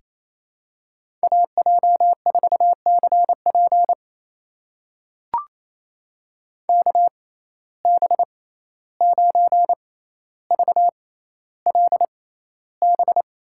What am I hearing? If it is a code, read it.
Morse code, heard as a keyed tone, about 28 wpm, spaced as for 14 wpm, 700 Hz. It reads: AJ4CP KB9VLB
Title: USA Call Signs - 6x Character Spacing - 28wpm